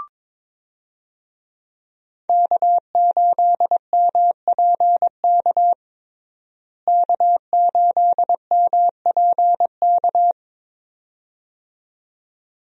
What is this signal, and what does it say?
Morse code, heard as a keyed tone, about 22 wpm, 700 Hz. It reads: K8MPK K8MPK
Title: USA Call Signs - 5 Characters - Repeated 2x - 22wpm